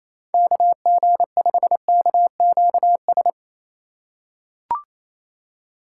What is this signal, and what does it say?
Morse code, heard as a keyed tone, about 28 wpm, 700 Hz. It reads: KG5KQS
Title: USA Call Signs - 2x Character Spacing - 28wpm